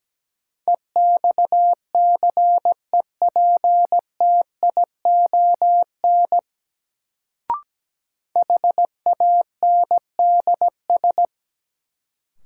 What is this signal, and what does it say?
Morse code, heard as a keyed tone, about 17 wpm, 700 Hz. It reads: EXCEPTION HANDS
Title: Sets of 4 Words - New Word Drill - 17wpm